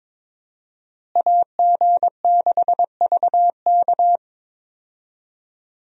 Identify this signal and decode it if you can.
Morse code, heard as a keyed tone, about 22 wpm, 700 Hz. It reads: AG6VK
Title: USA Call Signs - 5 Characters - Repeated 3x - 22wpm